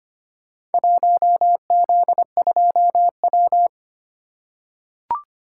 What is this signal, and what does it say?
Morse code, heard as a keyed tone, about 25 wpm, 700 Hz. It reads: 1Z2W
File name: practice / 25wpm / Four Letter-Number Combinations - 25wpm